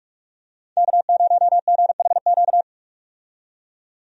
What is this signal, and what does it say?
Morse code, heard as a keyed tone, about 46 wpm, 700 Hz. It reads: K0GHQ
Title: USA Call Signs - 5 Characters - 45wpm